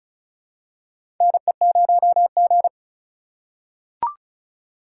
Morse code, heard as a keyed tone, about 34 wpm, 700 Hz. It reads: NE0G
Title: USA Call Signs - 4 Characters - 4x Character Spacing - 35wpm